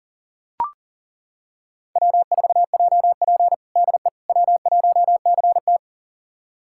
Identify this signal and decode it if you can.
Morse code, heard as a keyed tone, about 40 wpm, 700 Hz. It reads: W4JP DE W1CT
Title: USA Callsign DE Callsign - 4 Characters - 40wpm